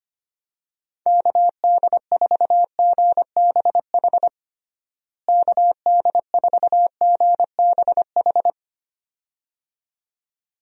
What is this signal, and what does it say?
Morse code, heard as a keyed tone, about 25 wpm, 700 Hz. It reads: KD4GBH KD4GBH
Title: USA Call Signs - 6 Characters - Repeated 2x - 25wpm